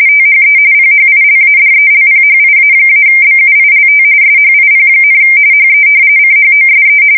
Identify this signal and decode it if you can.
RTTY, about 45.45 baud. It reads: THE PHILADELPHIA-BUILT SHIP, HE CHOSE
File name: fsk170 / battleshipnj